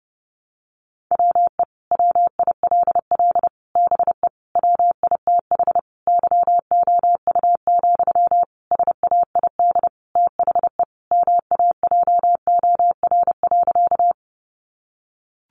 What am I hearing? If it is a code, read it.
Morse code, heard as a keyed tone, about 30 wpm, 700 Hz. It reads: WE WILL BE WITH YOU, SAID THE MAJOR.
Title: Sentences from Top 600 Words 30wpm